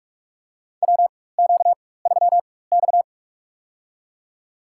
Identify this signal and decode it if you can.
Morse code, heard as a keyed tone, about 46 wpm, spaced as for 21 wpm, 700 Hz. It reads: WQ3X
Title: USA Call Signs - 4 Characters - 4x Character Spacing - 45wpm